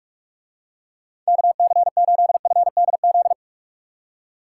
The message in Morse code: KC8FDZ

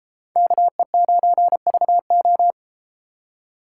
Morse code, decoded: KE9VO

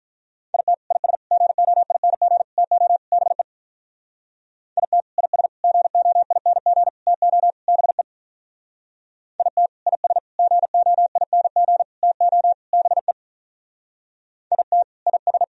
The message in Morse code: IT IS GOING TO BE IT IS GOING TO BE IT IS GOING TO BE IT IS

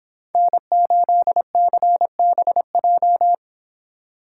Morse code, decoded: N8CBJ

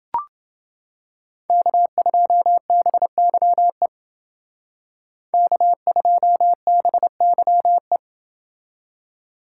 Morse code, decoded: K2BYE K2BYE

